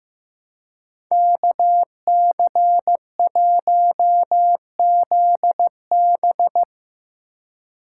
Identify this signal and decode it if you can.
Morse code, heard as a keyed tone, about 15 wpm, 700 Hz. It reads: KC1ZB
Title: USA Call Signs - 5 Characters - 2x Character Spacing - 15wpm